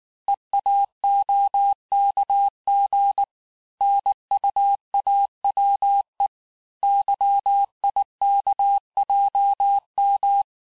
Morse code, heard as a keyed tone, about 19 wpm, 800 Hz. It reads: EAOKG NUAWE YIKJM
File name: cours-du-lundi files / cours-1595-teste-CHIFFRE-vitesse18-MPM